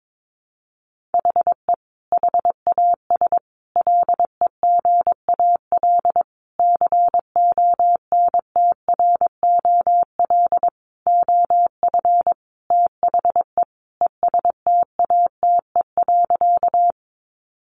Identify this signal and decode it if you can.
Morse code, heard as a keyed tone, about 22 wpm, 700 Hz. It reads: HE HAS LEGAL CONTROL OF THE ESTATE.
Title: Sentences from Top 1200 Words 22wpm